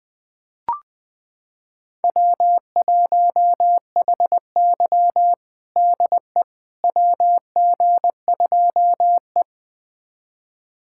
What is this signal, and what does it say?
Morse code, heard as a keyed tone, about 20 wpm, 700 Hz. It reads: W1HY DE WG2E
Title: USA Callsign DE Callsign - 4 Characters - 20wpm